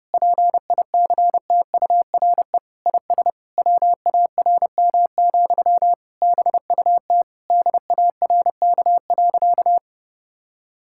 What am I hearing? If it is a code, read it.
Morse code, heard as a keyed tone, about 32 wpm, 700 Hz. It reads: PICTURE IS WARM, BUT DARK.
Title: Sentences from Top 1600 Words 30wpm